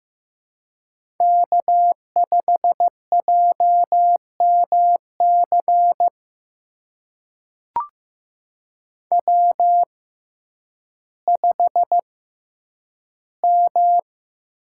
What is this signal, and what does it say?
Morse code, heard as a keyed tone, about 15 wpm, spaced as for 9 wpm, 700 Hz. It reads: K5JMC W5M
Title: USA Call Signs - 6x Character Spacing - 15wpm